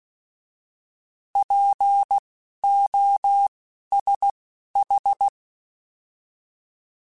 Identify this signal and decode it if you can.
Morse code, heard as a keyed tone, about 16 wpm, spaced as for 11 wpm, 800 Hz. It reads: POSH
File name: Alphabetical / Alphabetical mp3s / Ps @10